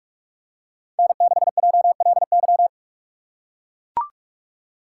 Morse code, heard as a keyed tone, about 44 wpm, 700 Hz. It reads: N6JLY